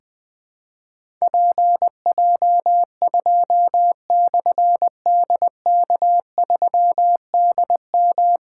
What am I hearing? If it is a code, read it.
Morse code, heard as a keyed tone, about 20 wpm, 700 Hz. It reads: PJ2/DK3DM
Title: Difficult Call Signs 20wpm